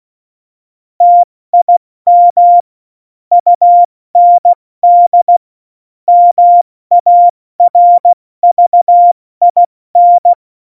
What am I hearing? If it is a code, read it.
Morse code, heard as a keyed tone, about 16 wpm, 700 Hz. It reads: TIM UND MARVIN
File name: Morsekurs / Witze / Witz 25. September 2018, 70 F80